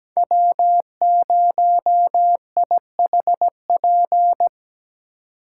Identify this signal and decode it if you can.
Morse code, heard as a keyed tone, about 20 wpm, 700 Hz. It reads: W0IHP
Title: Speed Racing - USA 1x3 Call Signs 17wpm